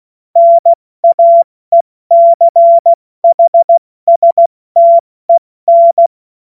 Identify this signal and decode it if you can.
Morse code, heard as a keyed tone, about 16 wpm, 680 Hz. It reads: NAECHSTEN
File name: Morsekurs / Witze / Witz 15. November 2022 70 F80